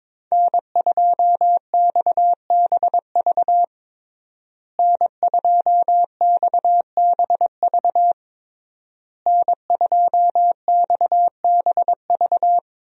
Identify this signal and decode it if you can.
Morse code, heard as a keyed tone, about 22 wpm, 700 Hz. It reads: N2XBV N2XBV N2XBV
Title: USA Call Signs - 5 Characters - Repeated 3x - 22wpm